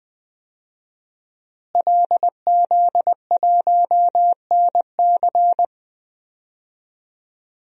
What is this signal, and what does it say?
Morse code, heard as a keyed tone, about 20 wpm, 700 Hz. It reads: LZ1NC
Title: DXCC-100 Call Signs - 5 Characters - Repeated 2x - 20wpm